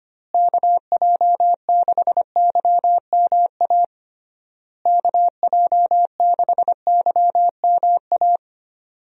Morse code, heard as a keyed tone, about 25 wpm, 700 Hz. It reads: KJ6YMA KJ6YMA
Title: USA Call Signs - 6 Characters - Repeated 3x - 25wpm